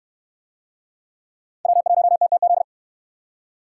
Morse code, heard as a keyed tone, about 68 wpm, 700 Hz. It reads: W1TTZ